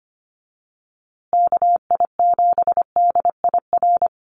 Morse code, heard as a keyed tone, about 25 wpm, 700 Hz. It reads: KI7DIR
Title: USA Call Signs Repeated 3x 25wpm